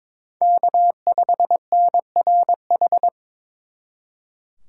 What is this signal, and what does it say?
Morse code, heard as a keyed tone, about 26 wpm, 700 Hz. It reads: K5NRH
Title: Speed Racing - USA 1x3 Call Signs 22wpm